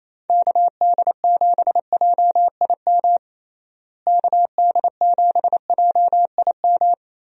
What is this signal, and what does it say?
Morse code, heard as a keyed tone, about 28 wpm, 700 Hz. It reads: KD7JIM KD7JIM
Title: USA Call Signs - 6 Characters - Repeated 3x - 28wpm